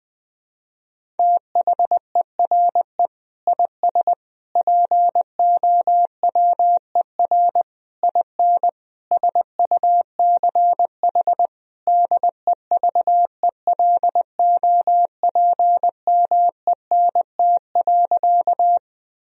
Morse code, heard as a keyed tone, about 21 wpm, 700 Hz. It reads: THERE IS POWER IN SUCH DEVELOPMENT.
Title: Sentences from Top 1000 Words - Review Part 4 - 20wpm